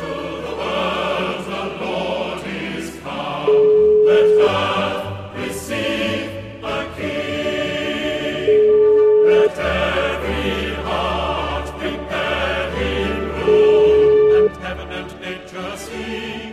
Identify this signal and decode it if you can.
Ringback tone (Europe, 425 Hz), with noise.